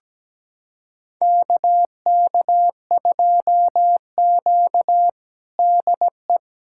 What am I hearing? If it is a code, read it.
Morse code, heard as a keyed tone, about 17 wpm, 700 Hz. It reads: KK2Q DE
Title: USA Callsign DE Callsign - 4 Characters - 17wpm